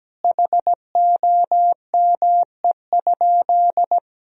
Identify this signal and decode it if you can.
Morse code, heard as a keyed tone, about 17 wpm, 700 Hz. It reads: HOME?
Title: Sentences from Top 1000 Words - Review Part 3 - 17wpm